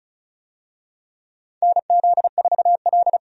Morse code, heard as a keyed tone, about 34 wpm, 700 Hz. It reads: NZ4L
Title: USA Call Signs - 4 Characters - 35wpm